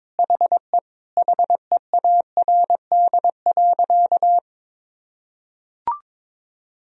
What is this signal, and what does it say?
Morse code, heard as a keyed tone, about 22 wpm, 700 Hz. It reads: HE HEARD.
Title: Sentences from Top 1000 Words - Review Part 1 - 22wpm